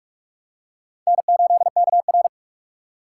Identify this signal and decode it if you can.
Morse code, heard as a keyed tone, about 46 wpm, 700 Hz. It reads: N8KR
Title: USA Call Signs - 4x Character Spacing - 45wpm